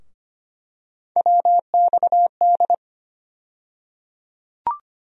Morse code, heard as a keyed tone, about 25 wpm, 700 Hz. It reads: WXD
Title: USA Call Sign Postfixes - 3 Letters 25wpm